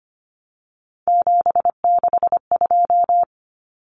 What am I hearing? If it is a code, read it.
Morse code, heard as a keyed tone, about 25 wpm, 700 Hz. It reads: 762
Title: Numbers - 3-Digit 25wpm